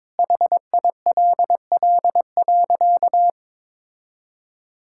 Morse code, heard as a keyed tone, about 22 wpm, 700 Hz. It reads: HILL.
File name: practice / 22wpm / Sentences with 4-Word Phrases - 22wpm